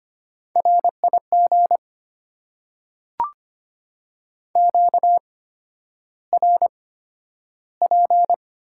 Morse code, heard as a keyed tone, about 25 wpm, spaced as for 11 wpm, 700 Hz. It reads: RIG QRP
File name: Top 90 QSO Elements - 8x Character Spacing - 25wpm